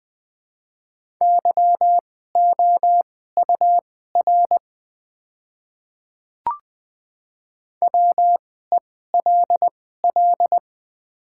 Morse code, heard as a keyed tone, about 21 wpm, spaced as for 15 wpm, 700 Hz. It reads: YOUR WELL
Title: 2x Character Spacing - Top 200 Words - 20wpm